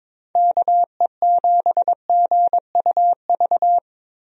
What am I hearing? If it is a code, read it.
Morse code, heard as a keyed tone, about 22 wpm, 700 Hz. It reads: KE7GUV
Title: USA Call Signs - 6 Characters - Repeated 2x - 22wpm